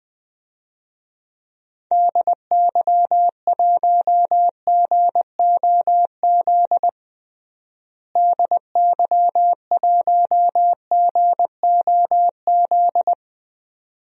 Morse code, heard as a keyed tone, about 20 wpm, 700 Hz. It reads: DY1GOZ DY1GOZ
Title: DXCC-100 Call Signs - 6 Characters - Repeated 2x - 20wpm